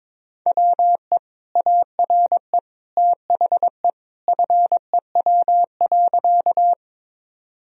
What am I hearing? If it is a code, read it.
Morse code, heard as a keyed tone, about 22 wpm, 700 Hz. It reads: WE ARE THE FEW.
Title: Sentences from Top 1000 Words - Review Part 1 - 22wpm